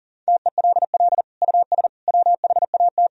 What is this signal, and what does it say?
Morse code, heard as a keyed tone, about 40 wpm, 700 Hz. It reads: TELL US WHAT
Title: Sentences from Top 1000 Words - Review Part 4 - 40wpm